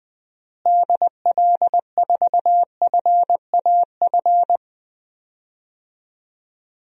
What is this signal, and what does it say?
Morse code, heard as a keyed tone, about 21 wpm, 700 Hz. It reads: DL4FAF
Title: DXCC-100 Call Signs - 6 Characters - Repeated 3x - 20wpm